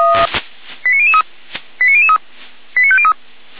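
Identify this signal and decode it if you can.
Five-tone selective call (ZVEI), heard as quick runs of five tones, tones of about 70 ms. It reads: 89003 89003 89503
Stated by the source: ZVEI five-tone call 89014 received by SDR, from the Feuerwehr Orlamünde audio server.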